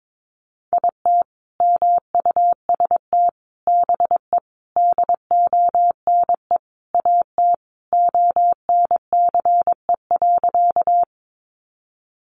Morse code, heard as a keyed tone, about 22 wpm, 700 Hz. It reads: IT MUST BE DONE AT ONCE.
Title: Sentences from Top 500 Words 22wpm